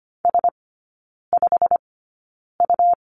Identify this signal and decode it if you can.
Morse code, heard as a keyed tone, about 25 wpm, spaced as for 9 wpm, 700 Hz. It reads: S5U
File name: Lesson 17e - Letter u - Three Characters 25wpm